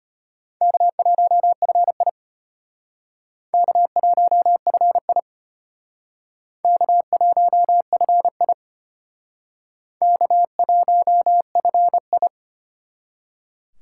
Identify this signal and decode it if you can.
Morse code, heard as a keyed tone, about 32 wpm, 700 Hz. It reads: K1FI K1FI K1FI K1FI